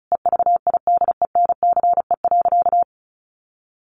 Morse code, heard as a keyed tone, about 34 wpm, 700 Hz. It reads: EVIDENCE.
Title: Sentences from Top 1300 Words 35wpm